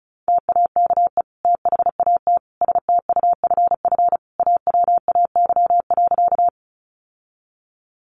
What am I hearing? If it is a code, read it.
Morse code, heard as a keyed tone, about 34 wpm, 700 Hz. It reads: TAKE THAT STUFF AWAY.
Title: Sentences from Top 1400 Words 35wpm